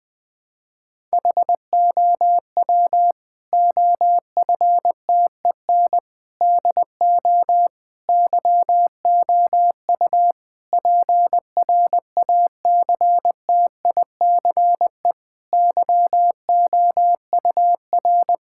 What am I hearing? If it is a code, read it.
Morse code, heard as a keyed tone, about 20 wpm, 700 Hz. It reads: HOW OFTEN DO YOU PRACTICE YOUR
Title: Sentences from Top 1000 Words - Review Part 6 - 20wpm